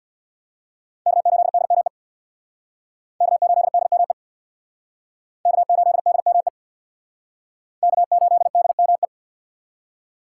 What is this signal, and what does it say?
Morse code, heard as a keyed tone, about 61 wpm, 700 Hz. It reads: K8DME K8DME K8DME K8DME
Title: Speed Racing - USA Call Signs 50wpm